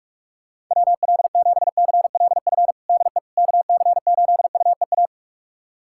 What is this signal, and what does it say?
Morse code, heard as a keyed tone, about 42 wpm, 700 Hz. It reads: WL7CLF DE KC8UEA
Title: USA Callsign DE Callsign - 6 Characters - 45wpm